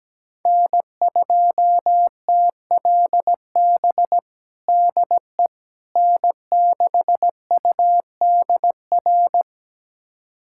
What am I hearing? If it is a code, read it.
Morse code, heard as a keyed tone, about 17 wpm, 700 Hz. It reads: N2TLB DE N6UDR
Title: USA Callsign DE Callsign - 5 Characters - 17wpm